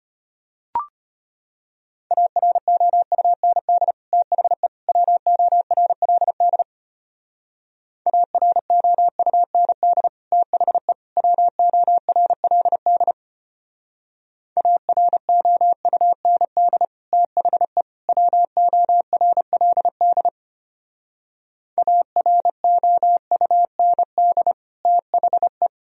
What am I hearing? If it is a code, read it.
Morse code, heard as a keyed tone, about 32 wpm, 700 Hz. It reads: AROUND THE WORLD AROUND THE WORLD AROUND THE WORLD AROUND THE